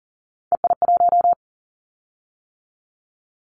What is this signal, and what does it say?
Morse code, heard as a keyed tone, about 40 wpm, 700 Hz. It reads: EI1
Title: Callsign Prefixes for Top 50 DXCC - 40wpm